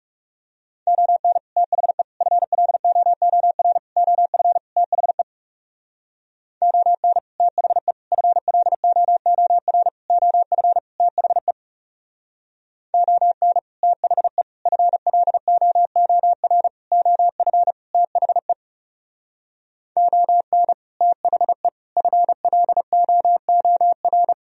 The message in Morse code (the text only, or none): ON THE FLOOR OF THE ON THE FLOOR OF THE ON THE FLOOR OF THE ON THE FLOOR